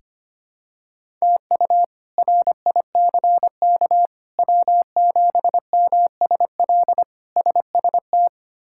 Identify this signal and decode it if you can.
Morse code, heard as a keyed tone, about 25 wpm, 700 Hz. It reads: TU RICK W7MSL SST
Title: K1USN SST QSO Practice - 25wpm